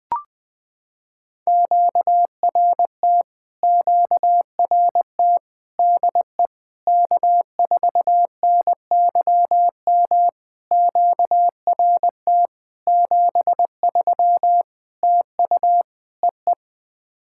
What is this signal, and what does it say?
Morse code, heard as a keyed tone, about 20 wpm, 700 Hz. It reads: QRT QRT DE K4NYM QRT 73 TU EE
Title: POTA QSO Practice - 20wpm